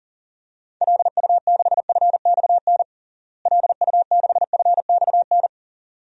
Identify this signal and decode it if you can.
Morse code, heard as a keyed tone, about 40 wpm, 700 Hz. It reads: LU6FXN LU6FXN
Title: DXCC-100 Call Signs - 6 Characters - Repeated 2x - 40wpm